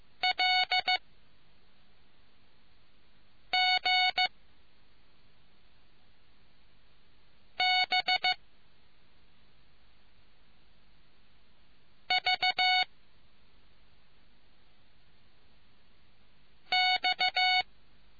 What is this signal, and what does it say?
Morse code, heard as a keyed tone, about 15 wpm, 740 Hz. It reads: L G B V X